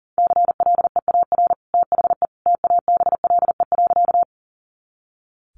Morse code, heard as a keyed tone, about 40 wpm, 700 Hz. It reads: CLEAR THE TABLE.
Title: Sentences from Top 500 Words 40wpm